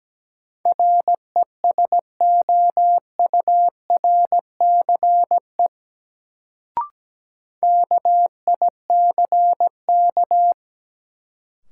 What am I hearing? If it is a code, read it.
Morse code, heard as a keyed tone, about 17 wpm, 700 Hz. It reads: RESOURCE KICK
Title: Sentences from Top 1300 Words - New Word Drill Alt - 17wpm